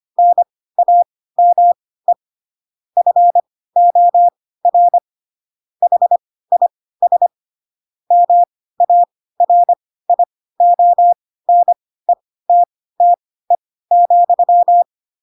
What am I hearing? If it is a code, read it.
Morse code, heard as a keyed tone, about 25 wpm, spaced as for 16 wpm, 700 Hz. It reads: NAME FOR HIS MARIONETTE,
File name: Pinocchio / Pinocchio ch3 10 25 ik1hge